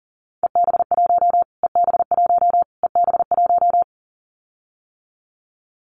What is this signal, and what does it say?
Morse code, heard as a keyed tone, about 40 wpm, 700 Hz. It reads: EB1 EB1 EB1
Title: Callsign Prefixes for Top 25 DXCC - Repeated 3x - 40wpm